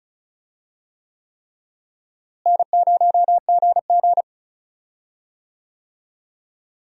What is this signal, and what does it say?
Morse code, heard as a keyed tone, about 36 wpm, 700 Hz. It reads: N0GG
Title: USA Call Signs - 4 Characters - Repeated 3x - 35wpm